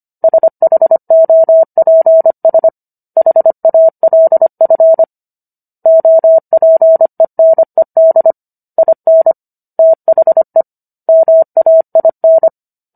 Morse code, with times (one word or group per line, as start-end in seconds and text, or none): 0.2-2.7 SHOPS
3.2-5.0 HALF
5.9-8.3 OPENED
8.8-9.3 IN
9.8-10.6 THE
11.1-12.5 MAIN